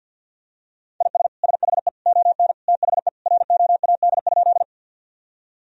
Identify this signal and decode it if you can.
Morse code, heard as a keyed tone, about 53 wpm, 700 Hz. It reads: IS SHE ON THE ROAD?
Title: Sentences from Top 1000 Words - Review Part 2 - 50wpm